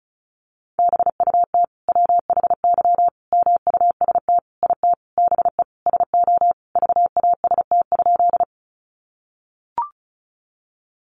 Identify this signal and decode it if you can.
Morse code, heard as a keyed tone, about 34 wpm, 700 Hz. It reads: BUT WHY MUST IT BE SO VAST?
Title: Sentences from Top 1300 Words 35wpm